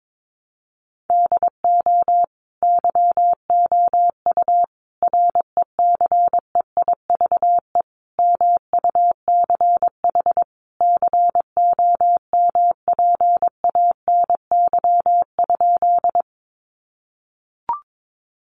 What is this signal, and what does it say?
Morse code, heard as a keyed tone, about 22 wpm, 700 Hz. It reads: DO YOU RECEIVE MUCH COMPANY?
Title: Sentences from Top 1000 Words 22wpm